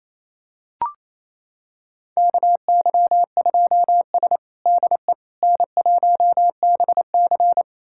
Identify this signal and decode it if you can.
Morse code, heard as a keyed tone, about 28 wpm, 700 Hz. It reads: KY2S DE N1BC
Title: USA Callsign DE Callsign - 4 Characters - 28wpm